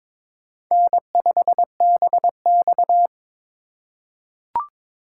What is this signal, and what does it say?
Morse code, heard as a keyed tone, about 22 wpm, 700 Hz. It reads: N5BX
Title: USA Call Signs - 4 Characters - 3x Character Spacing - 22wpm